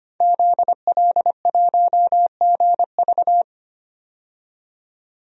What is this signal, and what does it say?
Morse code, heard as a keyed tone, about 25 wpm, 700 Hz. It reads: ZL1GV
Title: DXCC-100 Call Signs - 5 Characters - 25wpm